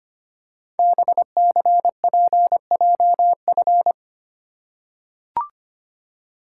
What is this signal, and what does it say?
Morse code, heard as a keyed tone, about 25 wpm, 700 Hz. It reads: BCPJF
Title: Code Groups - 5 Letters - 25wpm